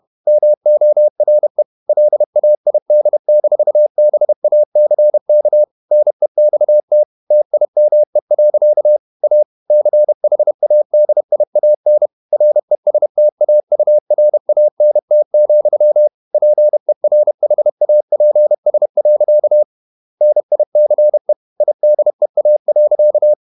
Morse code, heard as a keyed tone, about 31 wpm, 600 Hz. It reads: MORE LAID-BACK NEXT TIME. A CHADIAN RESTAURANT, PERHAPS. NICE IDEA.
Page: Never 31wpm | CW med Gnister